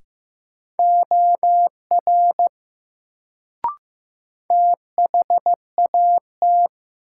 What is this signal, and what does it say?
Morse code, heard as a keyed tone, about 15 wpm, 700 Hz. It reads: OR THAT
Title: Sentences from Top 100 Words - New Word Drill - 15wpm